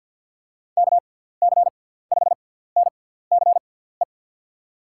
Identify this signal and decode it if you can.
Morse code, heard as a keyed tone, about 51 wpm, spaced as for 17 wpm, 700 Hz. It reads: KC5NCE
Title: USA Call Signs - 6 Characters - 6x Character Spacing - 50wpm